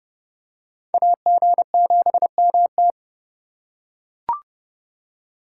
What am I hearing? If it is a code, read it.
Morse code, heard as a keyed tone, about 30 wpm, 700 Hz. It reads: AG7MT